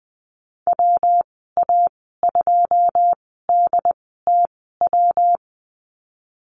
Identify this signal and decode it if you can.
Morse code, heard as a keyed tone, about 20 wpm, spaced as for 15 wpm, 700 Hz. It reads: WA2DTW
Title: 167 Lesson 27g - Number 7 - Call Signs 20wpm